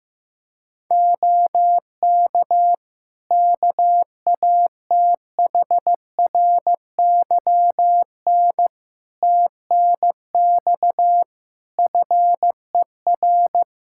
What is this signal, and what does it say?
Morse code, heard as a keyed tone, about 14 wpm, 700 Hz. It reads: OK KATHRYN TNX FER
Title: K1USN SST QSO Practice 15wpm